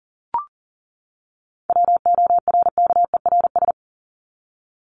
Morse code, heard as a keyed tone, about 40 wpm, 700 Hz. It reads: WORKERS
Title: ES:S Suffix - Top 100 Words with Base Words 40wpm